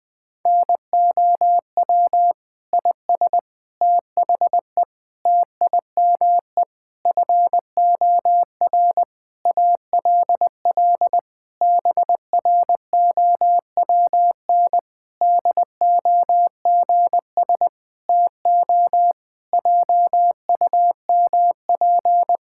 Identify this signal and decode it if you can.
Morse code, heard as a keyed tone, about 20 wpm, 700 Hz. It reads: NOW IS THE TIME FOR ALL BROWN DOGS TO JUMP
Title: Pangram Challenge v2 - 20wpm